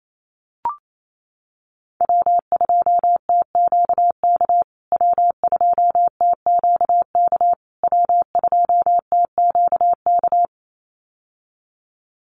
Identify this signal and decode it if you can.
Morse code, heard as a keyed tone, about 28 wpm, 690 Hz. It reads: W2TQK W2TQK W2TQK
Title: USA Call Signs Repeated 3x 28wpm